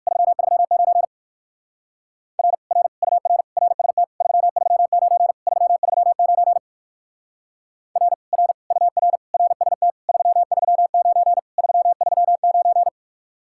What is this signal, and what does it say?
Morse code, heard as a keyed tone, about 51 wpm, 700 Hz. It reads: 339 R R UR RST 339 339 R R UR RST 339 339
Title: Speed-Racing Common QSO Phrases - 45wpm